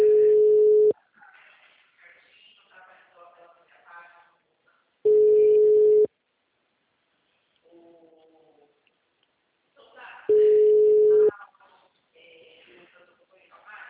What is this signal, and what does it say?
Ringback tone (Europe, 425 Hz), with speech.
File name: out- -2003- - - .2480